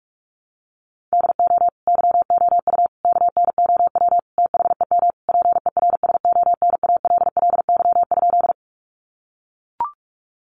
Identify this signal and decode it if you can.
Morse code, heard as a keyed tone, about 44 wpm, 700 Hz. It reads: DO YOU KNOW THEM PERSONALLY?
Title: Sentences from Top 1400 Words 45wpm